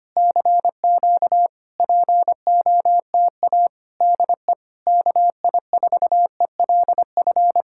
Morse code, heard as a keyed tone, about 25 wpm, 700 Hz. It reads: CQ POTA DE KI4ELF
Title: Speed-Racing Common QSO Phrases - 25wpm